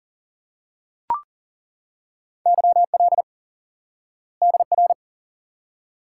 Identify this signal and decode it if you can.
Morse code, heard as a keyed tone, about 40 wpm, 700 Hz. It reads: YL DR
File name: IQE Course 3 - Lesson 89 - YAGI - Two Elements - 40wpm